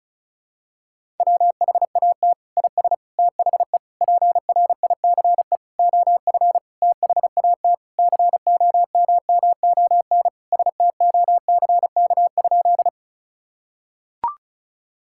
Morse code, heard as a keyed tone, about 36 wpm, 700 Hz. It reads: WHAT IS THE PRICE OF THAT COMMON STOCK?
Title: Sentences from Top 1000 Words - Review Part 5 - 35wpm